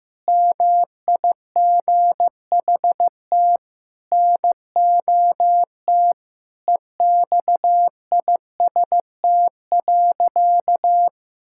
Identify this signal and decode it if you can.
Morse code, heard as a keyed tone, about 15 wpm, 700 Hz. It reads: MIGHT NOT EXIST.
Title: Sentences from Top 1000 Words - Review Part 5 - 15wpm